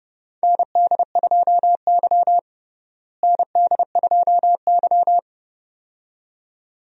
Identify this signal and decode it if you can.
Morse code, heard as a keyed tone, about 30 wpm, 700 Hz. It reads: ND2Y ND2Y
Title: USA Call Signs - 4 Characters - Repeated 2x - 30wpm